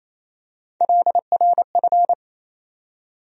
Morse code, heard as a keyed tone, about 28 wpm, 700 Hz. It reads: LRF
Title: USA Call Sign Postfixes - 3 Letters 28wpm